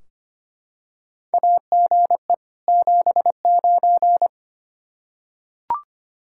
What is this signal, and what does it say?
Morse code, heard as a keyed tone, about 25 wpm, 700 Hz. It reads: AGE 79
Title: Practice QSO - Exchange 3 - More Info - 25wpm